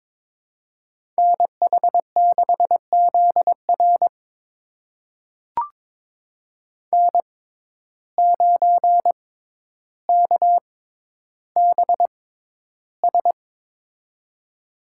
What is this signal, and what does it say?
Morse code, heard as a keyed tone, about 21 wpm, spaced as for 11 wpm, 700 Hz. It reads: NH6ZR N9KBS
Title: USA Call Signs - 5 Characters - 6x Character Spacing - 22wpm